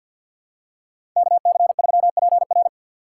Morse code, heard as a keyed tone, about 48 wpm, 700 Hz. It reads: KC3PR
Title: USA Call Signs - 5 Characters - 4x Character Spacing - 50wpm